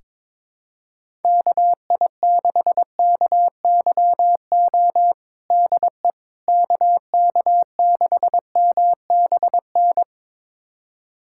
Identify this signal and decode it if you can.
Morse code, heard as a keyed tone, about 22 wpm, 700 Hz. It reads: KI6KYO DE KK6MBN
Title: USA Callsign DE Callsign - 6 Characters - 22wpm